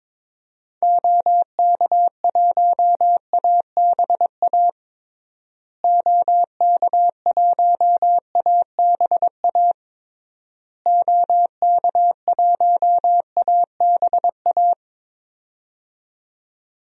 Morse code, heard as a keyed tone, about 22 wpm, 700 Hz. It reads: OK1ABA OK1ABA OK1ABA
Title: DXCC-100 Call Signs - 6 Characters - Repeated 3x - 22wpm